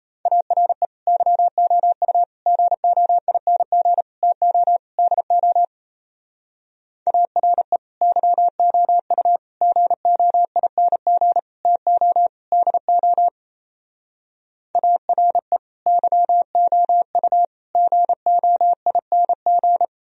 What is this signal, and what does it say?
Morse code, heard as a keyed tone, about 36 wpm, 700 Hz. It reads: ARE YOU GOING TO DO ARE YOU GOING TO DO ARE YOU GOING